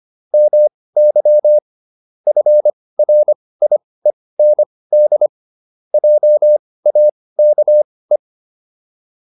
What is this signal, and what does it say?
Morse code, heard as a keyed tone, about 25 wpm, spaced as for 18 wpm, 600 Hz. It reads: MY FRIEND JAKE